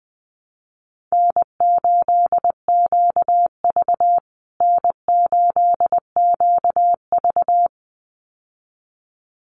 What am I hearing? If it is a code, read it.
Morse code, heard as a keyed tone, about 20 wpm, 700 Hz. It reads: N8QV N8QV
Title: USA Call Signs Repeated 2x 20wpm